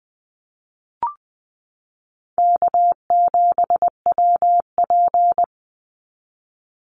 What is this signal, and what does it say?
Morse code, heard as a keyed tone, about 20 wpm, 700 Hz. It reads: K7WP
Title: Easy Call Signs Repeated 3x - 20wpm